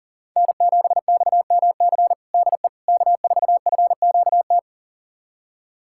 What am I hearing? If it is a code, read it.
Morse code, heard as a keyed tone, about 40 wpm, 700 Hz. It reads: N7XMC DE K4FQT
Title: USA Callsign DE Callsign 40wpm